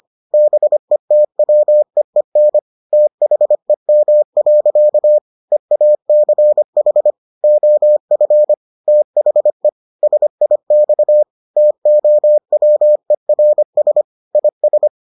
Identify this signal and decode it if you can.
Morse code, heard as a keyed tone, about 25 wpm, 600 Hz. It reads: BETWEEN THEM. EACH OF THE SIX TOWERS IS